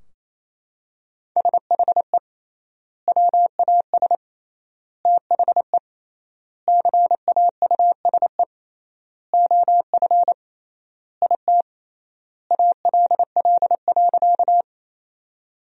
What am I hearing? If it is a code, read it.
Morse code, heard as a keyed tone, about 27 wpm, 700 Hz. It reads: SHE WAS THE CAUSE OF IT ALL.
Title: 3x Word-Spacing - Sentences from Top 200 Words - 28wpm